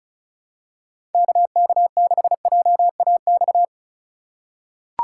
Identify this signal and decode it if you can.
Morse code, heard as a keyed tone, about 36 wpm, 700 Hz. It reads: KK6JAX
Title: USA Call Signs - 6 Characters - 3x Character Spacing - 35wpm